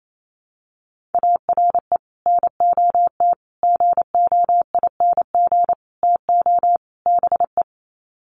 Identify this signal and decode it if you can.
Morse code, heard as a keyed tone, about 28 wpm, 700 Hz. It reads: ARE NOT GOING TO BE
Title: Sets of 5 Words Encore 28wpm